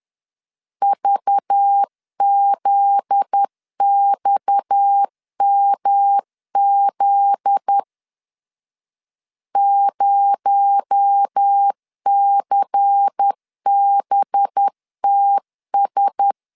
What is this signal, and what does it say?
Morse code, heard as a keyed tone, about 10 wpm, 800 Hz. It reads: VZXMZ 0CBTS
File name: LEZIONI / 8 YJ45 / 8 Y 10wpm 2615